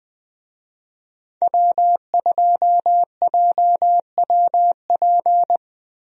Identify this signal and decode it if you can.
Morse code, heard as a keyed tone, about 21 wpm, 700 Hz. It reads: W2JWP
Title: USA Call Signs - 5 Characters - 20wpm